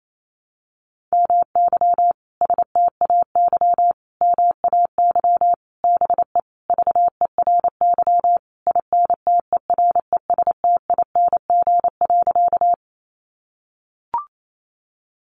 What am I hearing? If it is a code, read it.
Morse code, heard as a keyed tone, about 28 wpm, 700 Hz. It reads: MY STAY MAY BE VERY INTERESTING.
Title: Sentences from Top 1000 Words 28wpm